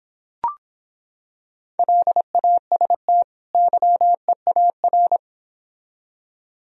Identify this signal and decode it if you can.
Morse code, heard as a keyed tone, about 26 wpm, 700 Hz. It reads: LAST YEAR